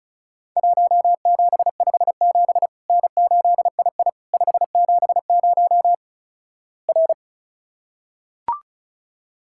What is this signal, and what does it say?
Morse code, heard as a keyed tone, about 34 wpm, 700 Hz. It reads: 1757 N8II 570 R
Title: Worked All Europe DX Contest - QTC Exchange - 35wpm